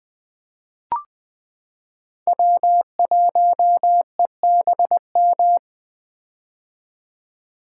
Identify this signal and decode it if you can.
Morse code, heard as a keyed tone, about 21 wpm, 700 Hz. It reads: W1EBM
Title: K1USN SST QSO Practice - 20wpm